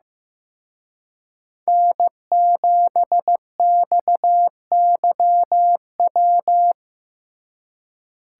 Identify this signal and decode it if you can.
Morse code, heard as a keyed tone, about 17 wpm, 700 Hz. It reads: N7XYW
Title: Speed Racing - USA 1x3 Call Signs 15wpm